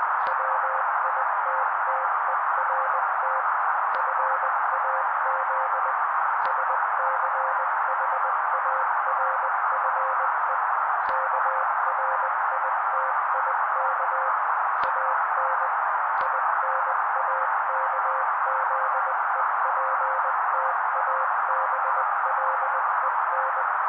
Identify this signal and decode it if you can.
Morse code, heard as a keyed tone, about 20 wpm, 530 Hz. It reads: WITTERT FAZ SCHARFE KRITIK AN INAKZEPTABLEN
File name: daily / 18wpm noise